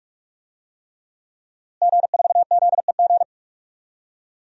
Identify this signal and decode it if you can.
Morse code, heard as a keyed tone, about 48 wpm, 700 Hz. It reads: G4ZEG
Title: DXCC-100 Call Signs - 5 Characters - Repeated 3x - 45wpm